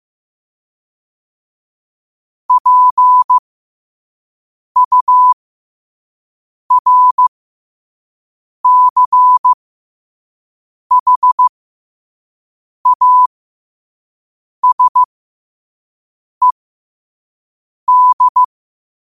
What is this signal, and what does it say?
Morse code, heard as a keyed tone, about 15 wpm, spaced as for 5 wpm, 1000 Hz. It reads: PURCHASED
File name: qotdfiles / 5wpm